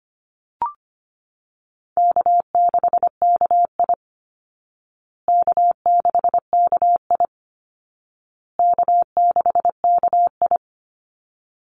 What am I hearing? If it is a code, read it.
Morse code, heard as a keyed tone, about 25 wpm, 700 Hz. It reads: K6KI K6KI K6KI
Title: Call Signs Repeated 2x - 25wpm